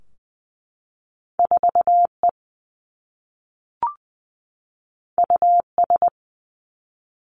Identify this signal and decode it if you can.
Morse code, heard as a keyed tone, about 20 wpm, 700 Hz. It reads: 4 E U S
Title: Two Challenging Letter-Number Combinations 20wpm